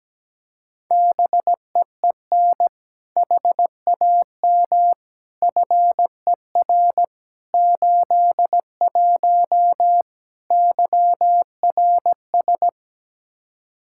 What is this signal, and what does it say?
Morse code, heard as a keyed tone, about 17 wpm, 700 Hz. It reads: BEEN HAM FER 81 YRS
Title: Practice QSO - Exchange 3 - More Info - 17wpm